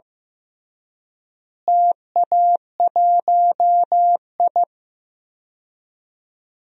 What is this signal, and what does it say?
Morse code, heard as a keyed tone, about 15 wpm, 700 Hz. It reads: TA1I
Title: DXCC-100 Call Signs - 4 Characters - Repeated 3x - 15wpm